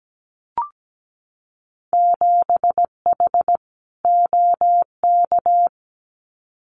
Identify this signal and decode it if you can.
Morse code, heard as a keyed tone, about 17 wpm, 700 Hz. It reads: 7H OK
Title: Winter Field Day Exchanges Encore 17wpm